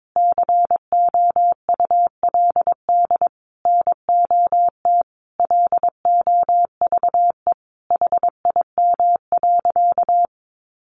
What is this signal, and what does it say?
Morse code, heard as a keyed tone, about 22 wpm, 700 Hz. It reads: COULD NOT LOVE HIM.
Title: Sentences from Top 600 Words 22wpm